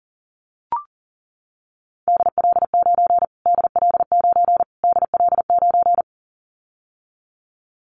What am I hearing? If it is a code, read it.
Morse code, heard as a keyed tone, about 40 wpm, 700 Hz. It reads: DL9 DL9 DL9
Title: Callsign Prefixes for Top 25 DXCC - Repeated 3x - 40wpm